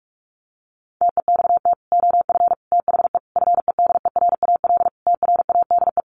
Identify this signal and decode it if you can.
Morse code, heard as a keyed tone, about 44 wpm, 700 Hz. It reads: TEXT OF THE FEDERAL TRADE
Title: Sentences from Top 1400 Words 45wpm